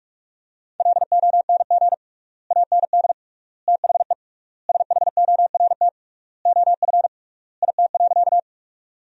Morse code, heard as a keyed tone, about 46 wpm, 700 Hz. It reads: LONG AND THE SHORT OF IT.
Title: 3x Word-Spacing - Sentences from Top 400 Words - 45wpm